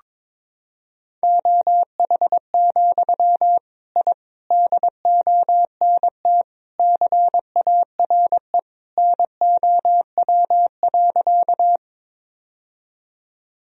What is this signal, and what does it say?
Morse code, heard as a keyed tone, about 22 wpm, 700 Hz. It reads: OH, I DONT CARE NOW.
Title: Sentences from Top 1000 Words - Review Part 1 - 22wpm